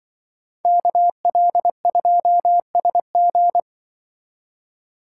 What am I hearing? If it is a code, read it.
Morse code, heard as a keyed tone, about 24 wpm, 700 Hz. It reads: KL2SG
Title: Speed Racing - USA Call Signs 20wpm